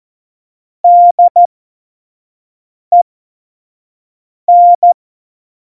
Morse code, heard as a keyed tone, about 14 wpm, spaced as for 5 wpm, 700 Hz. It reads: DEN
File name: Morsekurs / Witze / Witz 23. Oktober 2018, 25 F70